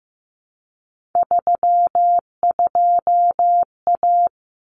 Morse code, heard as a keyed tone, about 15 wpm, 700 Hz. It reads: 32A
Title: Common Cut Numbers - 3-Digit 15wpm